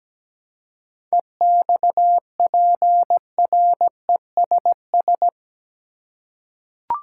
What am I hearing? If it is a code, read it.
Morse code, heard as a keyed tone, about 17 wpm, 700 Hz. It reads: EXPRESS
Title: Seven Letter Words - 17wpm